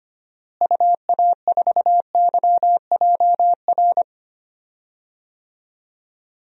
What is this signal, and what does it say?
Morse code, heard as a keyed tone, about 25 wpm, 700 Hz. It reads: UA4YJR
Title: DXCC-100 Call Signs - 6 Characters - Repeated 3x - 25wpm